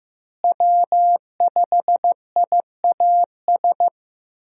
Morse code, heard as a keyed tone, about 15 wpm, 700 Hz. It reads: W5IAS
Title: USA Call Signs - 5 Characters - Repeated 3x - 15wpm